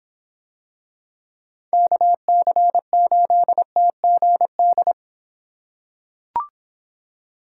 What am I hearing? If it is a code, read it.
Morse code, heard as a keyed tone, about 26 wpm, 700 Hz. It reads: KC8TGD